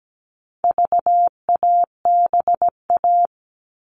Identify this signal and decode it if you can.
Morse code, heard as a keyed tone, about 17 wpm, 700 Hz. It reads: VABA
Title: All Cut Numbers - 4-Digit 17wpm